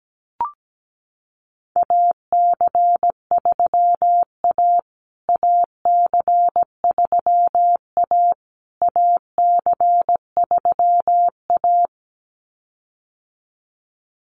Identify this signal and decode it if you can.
Morse code, heard as a keyed tone, about 17 wpm, 700 Hz. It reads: AC3A AC3A AC3A
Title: USA Call Signs Repeated 3x 17wpm